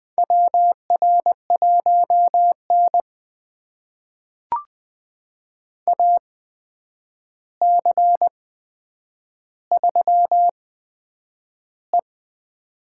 Morse code, heard as a keyed tone, about 20 wpm, spaced as for 9 wpm, 700 Hz. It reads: WR1N AC3E
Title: USA Call Signs - 4 Characters - 8x Character Spacing - 20wpm